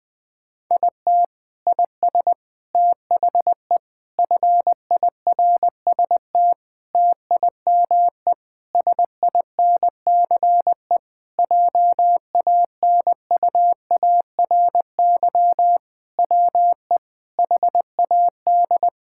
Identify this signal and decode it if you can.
Morse code, heard as a keyed tone, about 20 wpm, 700 Hz. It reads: IT IS THE FIRST TIME SINCE JANUARY WE HAD
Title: Sentences with 4-Word Phrases - 20wpm